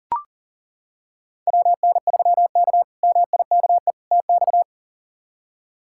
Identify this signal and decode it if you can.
Morse code, heard as a keyed tone, about 40 wpm, 700 Hz. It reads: WN3K MIKE TX
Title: K1USN SST QSO Practice - 40wpm